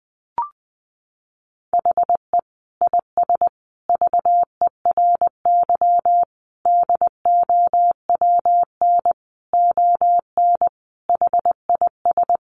Morse code, heard as a keyed tone, about 20 wpm, 700 Hz. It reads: HE IS VERY DOWN ON HIS
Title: Sentences from Top 1500 Words 20wpm